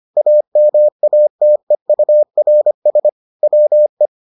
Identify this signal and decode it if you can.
Morse code, heard as a keyed tone, about 25 wpm, 600 Hz. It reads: AMATEURS WE